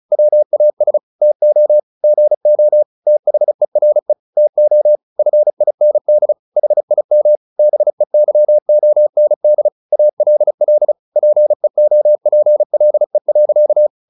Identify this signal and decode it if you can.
Morse code, heard as a keyed tone, about 34 wpm, 600 Hz. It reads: WAS TO GO THERE TO FIND HIM BEYOND ALL PEOPLE.
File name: the old man and the sea / the old man and the sea 1 35wpm